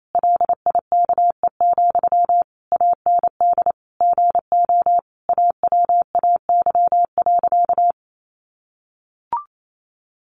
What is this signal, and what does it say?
Morse code, heard as a keyed tone, about 28 wpm, 700 Hz. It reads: LIKE, AND GO AWAY.
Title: Sentences from Top 1100 Words 28wpm